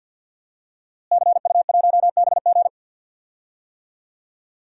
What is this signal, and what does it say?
Morse code, heard as a keyed tone, about 50 wpm, 700 Hz. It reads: CU1BG
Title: DXCC-100 Call Signs - 5 Characters - Repeated 2x - 50wpm